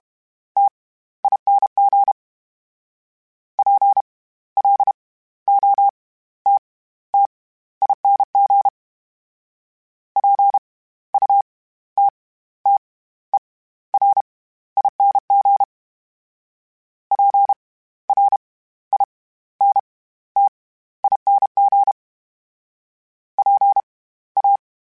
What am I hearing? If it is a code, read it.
Morse code, heard as a keyed tone, about 32 wpm, spaced as for 15 wpm, 800 Hz. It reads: TING PLOTTING PUTTERING PRINTING PA
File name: Special INGs / Special INGs-mp3s / SlowToFaster ing @30